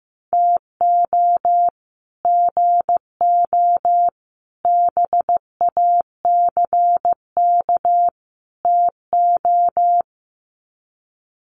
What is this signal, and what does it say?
Morse code, heard as a keyed tone, about 15 wpm, 700 Hz. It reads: TO GO BACK TO
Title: Sets of 4 Words 15wpm